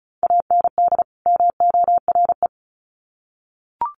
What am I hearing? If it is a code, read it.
Morse code, heard as a keyed tone, about 34 wpm, 700 Hz. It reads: AND MORE
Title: Sets of 2 Words 35wpm